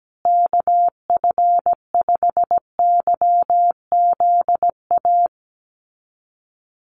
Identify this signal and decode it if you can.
Morse code, heard as a keyed tone, about 17 wpm, 700 Hz. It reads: KF5YZA
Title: USA Call Signs Repeated 3x 17wpm